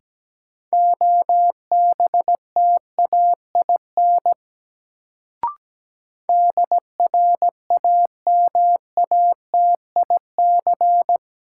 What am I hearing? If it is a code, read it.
Morse code, heard as a keyed tone, about 17 wpm, 700 Hz. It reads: OBTAIN DRAMATIC
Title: Sentences from Top 1300 Words - New Word Drill - 17wpm